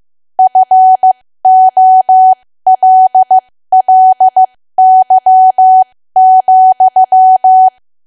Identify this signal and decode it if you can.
Morse code, heard as a keyed tone, about 15 wpm, spaced as for 13 wpm, 750 Hz. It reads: FOLLY,
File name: cwa / W1AW-Level2 / 13WPM